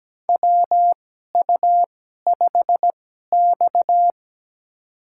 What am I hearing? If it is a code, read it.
Morse code, heard as a keyed tone, about 17 wpm, spaced as for 12 wpm, 700 Hz. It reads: WU5X
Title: USA Call Signs - 4 Characters - 2x Character Spacing - 17wpm